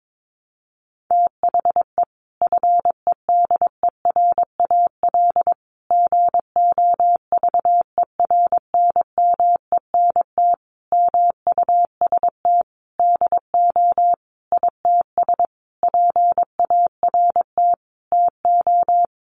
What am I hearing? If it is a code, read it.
Morse code, heard as a keyed tone, about 22 wpm, 700 Hz. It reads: THE FEDERAL GOVERNMENT MUST DO ITS PART TO
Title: Sentences from Top 1100 Words 22wpm